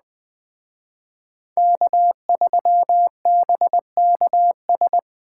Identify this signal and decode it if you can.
Morse code, heard as a keyed tone, about 21 wpm, 700 Hz. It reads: K3BKS
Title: Speed Racing - USA 1x3 Call Signs 17wpm